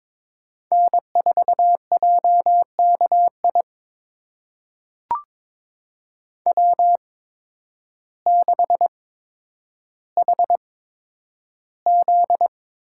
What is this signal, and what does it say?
Morse code, heard as a keyed tone, about 22 wpm, spaced as for 10 wpm, 700 Hz. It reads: N4JKI W6HZ
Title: USA Call Signs - 5 Characters - 8x Character Spacing - 22wpm